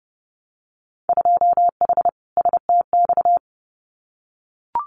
Morse code, heard as a keyed tone, about 30 wpm, 700 Hz. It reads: 2H STX
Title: Winter Field Day Exchanges Encore 30wpm